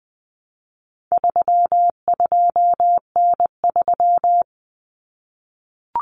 Morse code, heard as a keyed tone, about 20 wpm, 700 Hz. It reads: 32N3
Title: Common Cut Numbers - 4-Digit 20wpm